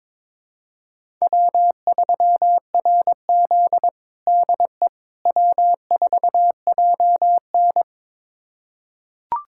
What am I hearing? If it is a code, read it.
Morse code, heard as a keyed tone, about 22 wpm, 700 Hz. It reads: W3RZ DE W4JN
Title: USA Callsign DE Callsign - 4 Characters - 22wpm